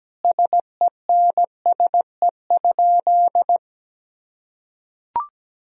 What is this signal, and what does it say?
Morse code, heard as a keyed tone, about 17 wpm, 700 Hz. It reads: SENSE?
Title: Sentences from Top 1000 Words - Review Part 4 - 17wpm